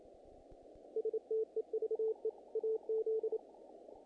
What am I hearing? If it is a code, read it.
Morse code, heard as a keyed tone, about 29 wpm, 470 Hz. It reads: STEVE AZ
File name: contests / naqp-cw-k5tr-op-k5pi- -1800